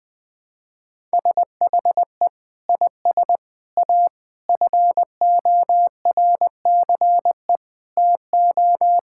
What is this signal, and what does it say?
Morse code, heard as a keyed tone, about 20 wpm, 700 Hz. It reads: SHE IS A FORCE TO
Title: Sentences with 6-Word Phrases - 20wpm